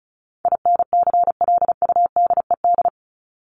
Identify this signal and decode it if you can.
Morse code, heard as a keyed tone, about 34 wpm, 700 Hz. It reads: INCLUDED
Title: ED Suffix - Top 100 Words with Base Words 35wpm